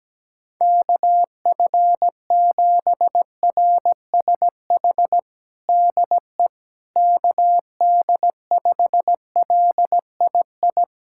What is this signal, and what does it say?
Morse code, heard as a keyed tone, about 17 wpm, 700 Hz. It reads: KF7RSH DE KD5LII
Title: USA Callsign DE Callsign 17wpm